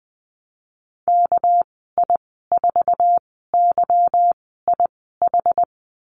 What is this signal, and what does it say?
Morse code, heard as a keyed tone, about 20 wpm, spaced as for 14 wpm, 700 Hz. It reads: KI4YIH
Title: 213 Lesson 34g - Number 8 - Call Signs 20wpm